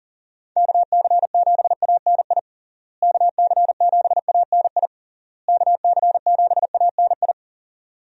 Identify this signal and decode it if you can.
Morse code, heard as a keyed tone, about 37 wpm, 700 Hz. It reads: KC7ANI KC7ANI KC7ANI
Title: USA Call Signs - 6 Characters - Repeated 3x - 40wpm